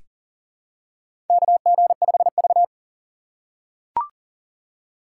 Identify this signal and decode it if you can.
Morse code, heard as a keyed tone, about 39 wpm, 700 Hz. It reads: KG5V
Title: K1USN SST QSO Practice - 40wpm